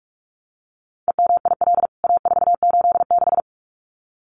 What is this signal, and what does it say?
Morse code, heard as a keyed tone, about 44 wpm, 700 Hz. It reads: EMIL A486
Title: CWT Exchange 45wpm